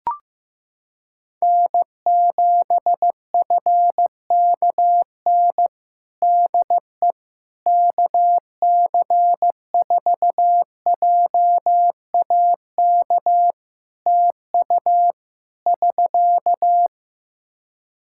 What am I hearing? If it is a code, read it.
Morse code, heard as a keyed tone, about 15 wpm, 700 Hz. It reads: N7FKN DE KC4JAK TU <SK>
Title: Practice QSO - Exchange 4 - Ending - 15wpm